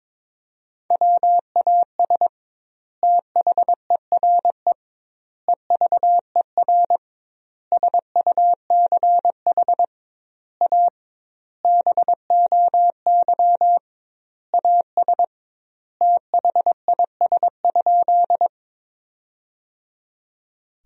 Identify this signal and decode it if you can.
Morse code, heard as a keyed tone, about 22 wpm, 700 Hz. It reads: WAS THERE EVER SUCH A BOY AS THIS?
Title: 2x Word-Spacing - Sentences from Top 400 Words - 22wpm